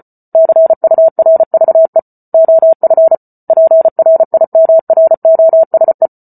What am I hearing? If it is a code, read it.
Morse code, heard as a keyed tone, about 34 wpm, 650 Hz. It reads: CURVE OF PRIMROSE
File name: Book Two / 08-Chapter 8 - 34 WPM